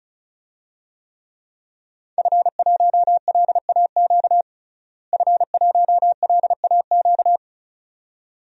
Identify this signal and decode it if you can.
Morse code, heard as a keyed tone, about 36 wpm, 700 Hz. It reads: F1LAQ F1LAQ
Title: DXCC-100 Call Signs - 5 Characters - Repeated 2x - 35wpm